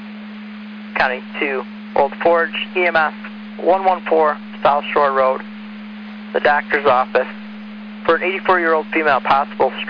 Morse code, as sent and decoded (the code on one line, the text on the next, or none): none